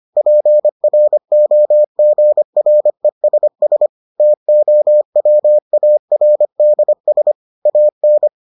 Morse code, heard as a keyed tone, about 25 wpm, 600 Hz. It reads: PROGRESS TOWARDS AN